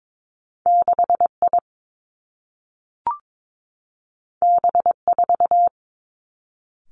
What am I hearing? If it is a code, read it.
Morse code, heard as a keyed tone, about 22 wpm, 700 Hz. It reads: 6 I B 4
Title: Two Challenging Letter-Number Combinations 22wpm